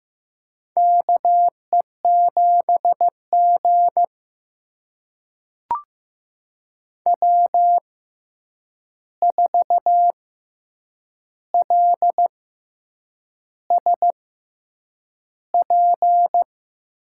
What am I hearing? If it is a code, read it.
Morse code, heard as a keyed tone, about 15 wpm, spaced as for 7 wpm, 700 Hz. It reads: KE7G W4LSP
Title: USA Call Signs - 6x Character Spacing - 15wpm